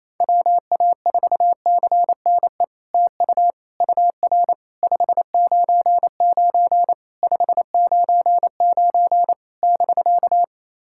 Morse code, heard as a keyed tone, about 28 wpm, 700 Hz. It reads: WA4CNE TU UR 599 599 <BK>
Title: SOTA QSO Practice 28wpm